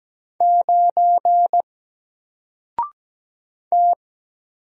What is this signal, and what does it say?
Morse code, heard as a keyed tone, about 17 wpm, 700 Hz. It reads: 9 T
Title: Single Character (Letter:Number:Punctuation:Prosign) Alt 17wpm